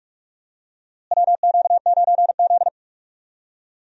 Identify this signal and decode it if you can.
Morse code, heard as a keyed tone, about 46 wpm, 700 Hz. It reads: WQ9Z
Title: USA Call Signs - 4 Characters - Repeated 3x - 45wpm